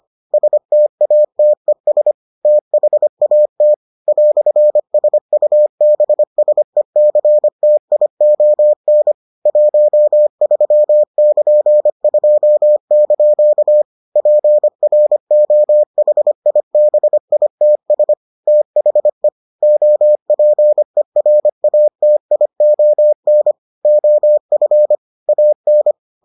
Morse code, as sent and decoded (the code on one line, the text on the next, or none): ... - .- - . ... / - .... .- - / .-..-. ... ..- -... ... -.-. - .. --- -. / .---- ...-- -.--. ..--- -.--.- / .--. .-. --- .... .. -... .. - ... / - .... / --- .--. .-. .- - .. --- -. / --- ..-. / .- -.
STATES THAT "SUBSECTION 13<KN>2) PROHIBITS THE OPERATION OF AN